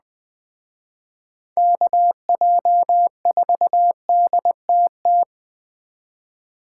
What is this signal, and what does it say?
Morse code, heard as a keyed tone, about 20 wpm, 700 Hz. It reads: KJ4DTT
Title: Speed Racing - USA Call Signs 17wpm